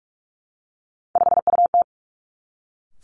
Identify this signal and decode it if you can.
Morse code, heard as a keyed tone, about 46 wpm, 700 Hz. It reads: 5UT
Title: Three Challenging Letter-Number Combinations 45wpm